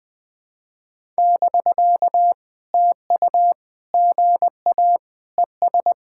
Morse code, heard as a keyed tone, about 20 wpm, 700 Hz. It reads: <BK> TU GA ES
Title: POTA QSO Practice - 20wpm